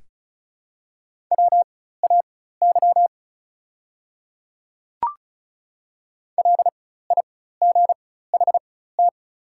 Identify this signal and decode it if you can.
Morse code, heard as a keyed tone, about 34 wpm, spaced as for 16 wpm, 700 Hz. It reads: WAY LIGHT
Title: 4x Character Spacing - Top 200 Words - 35wpm